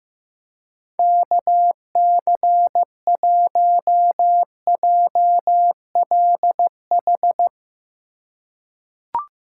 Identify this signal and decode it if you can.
Morse code, heard as a keyed tone, about 15 wpm, 700 Hz. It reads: KC1JLH
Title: USA Call Signs - 6 Characters - 3x Character Spacing - 15wpm